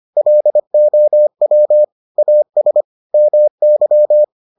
Morse code, heard as a keyed tone, about 25 wpm, 600 Hz. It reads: LOW AS MY